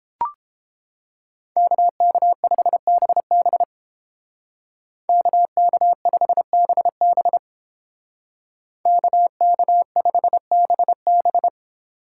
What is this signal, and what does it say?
Morse code, heard as a keyed tone, about 29 wpm, 700 Hz. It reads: KK5BB KK5BB KK5BB